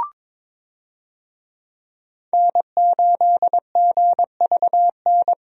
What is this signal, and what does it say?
Morse code, heard as a keyed tone, about 22 wpm, 700 Hz. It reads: N8GVN
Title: USA Call Signs - 5 Characters - 22wpm